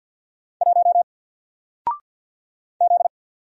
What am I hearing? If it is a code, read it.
Morse code, heard as a keyed tone, about 50 wpm, 700 Hz. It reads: 1 Z
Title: Single Letter-Number 50wpm